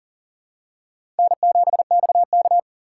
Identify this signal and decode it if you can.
Morse code, heard as a keyed tone, about 40 wpm, 700 Hz. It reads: N7XK
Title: USA Call Signs - 4 Characters - 2x Character Spacing - 40wpm